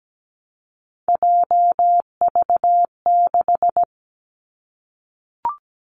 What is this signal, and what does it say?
Morse code, heard as a keyed tone, about 17 wpm, 700 Hz. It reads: JV6
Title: Three Challenging Letter-Number Combinations 17wpm